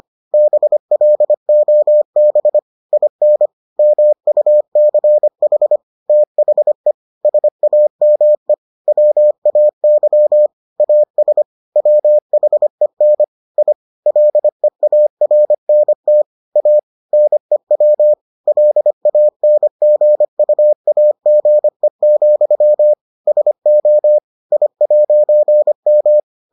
Morse code, heard as a keyed tone, about 25 wpm, 600 Hz. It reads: BLOB IN MUCH THE SAME WAY AS WHEN I LEARNT A NEW LANGUAGE, SO I'M